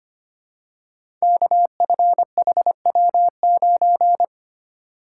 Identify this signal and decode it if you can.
Morse code, heard as a keyed tone, about 25 wpm, 700 Hz. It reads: KFHW9
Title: Code Groups - 5 Letter:Numbers - 25wpm